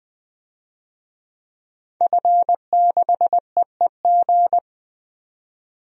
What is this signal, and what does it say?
Morse code, heard as a keyed tone, about 21 wpm, 700 Hz. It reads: F6EEG
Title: DXCC-100 Call Signs - 5 Characters - 20wpm